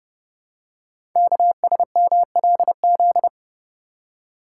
Morse code, heard as a keyed tone, about 29 wpm, 700 Hz. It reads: KSMLZ
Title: Code Groups - 5 Letters - 30wpm